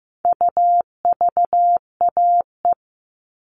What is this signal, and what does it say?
Morse code, heard as a keyed tone, about 15 wpm, 700 Hz. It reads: UVAE
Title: All Cut Numbers - 4-Digit 15wpm